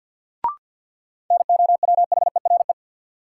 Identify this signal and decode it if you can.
Morse code, heard as a keyed tone, about 53 wpm, 700 Hz. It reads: NOWHERE
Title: Sets of 5 Words Encore - New Word Drill - 50wpm